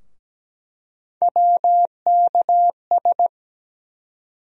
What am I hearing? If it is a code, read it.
Morse code, heard as a keyed tone, about 17 wpm, 700 Hz. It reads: WKS
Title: USA Call Sign Postfixes - 3 Letters 17wpm